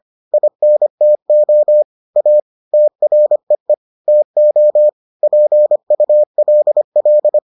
Morse code, heard as a keyed tone, about 25 wpm, 600 Hz. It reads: INTO A TREE TO PULL